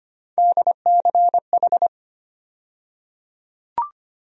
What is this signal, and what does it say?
Morse code, heard as a keyed tone, about 25 wpm, 700 Hz. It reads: DCH
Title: USA Call Sign Postfixes - 3 Letters 25wpm